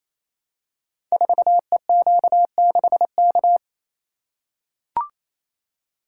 Morse code, heard as a keyed tone, about 29 wpm, 700 Hz. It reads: 4EQ6K
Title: Code Groups - 5 Letter:Numbers - 28wpm